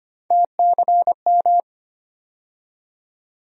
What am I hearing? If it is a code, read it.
Morse code, heard as a keyed tone, about 25 wpm, 700 Hz. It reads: TCM
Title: USA Call Sign Postfixes - 3 Letters 25wpm